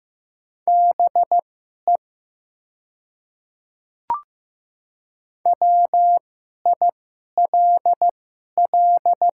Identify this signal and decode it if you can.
Morse code, heard as a keyed tone, about 15 wpm, spaced as for 11 wpm, 700 Hz. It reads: BE WILL
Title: 2x Character Spacing - Top 200 Words - 15wpm